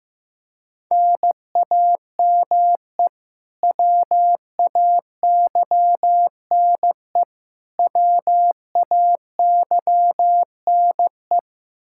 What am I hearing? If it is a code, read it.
Morse code, heard as a keyed tone, about 16 wpm, 700 Hz. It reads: NAME WAYNE WAYNE
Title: Practice QSO - Exchange 1 - Beginning - 15wpm